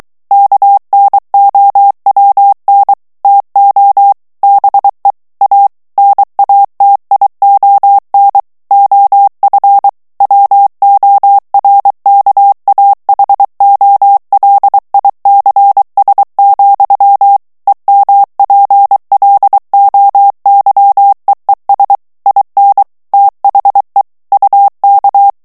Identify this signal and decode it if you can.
Morse code, heard as a keyed tone, about 23 wpm, 800 Hz. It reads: KNOWN TO BE A NATION OF WORKAHOLICS, EMPLOYEES IN THE UK
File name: morse mp3 / ANGLAIS texte clair 1 vitesse 22 mots minute